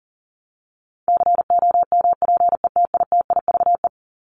Morse code, heard as a keyed tone, about 40 wpm, 700 Hz. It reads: COMPETITIVE
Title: Top 1001-1250 Words 40wpm